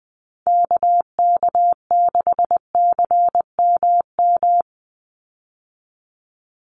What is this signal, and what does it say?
Morse code, heard as a keyed tone, about 20 wpm, 700 Hz. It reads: KK6CMM
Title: USA Call Signs 20wpm